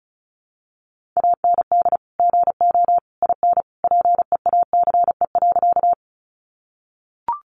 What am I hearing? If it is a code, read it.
Morse code, heard as a keyed tone, about 34 wpm, 700 Hz. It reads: AND GO IN PEACE.
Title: Sentences from Top 1500 Words 35wpm